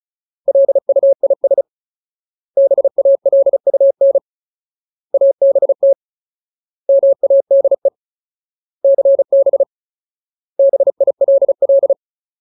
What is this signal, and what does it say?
Morse code, heard as a keyed tone, about 34 wpm, 550 Hz. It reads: LUIS BALUN ABT MADE CB BILL
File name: Ultimate Word List / 35WPM Ultimate